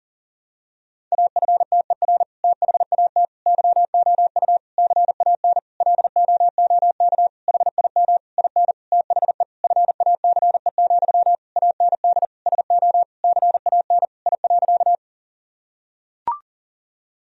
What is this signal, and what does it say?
Morse code, heard as a keyed tone, about 40 wpm, 700 Hz. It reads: AFTER THAT YOU CAN LOOK HIM IN THE FACE, AND SO CAN I.
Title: Sentences from Top 1000 Words - Review Part 2 - 40wpm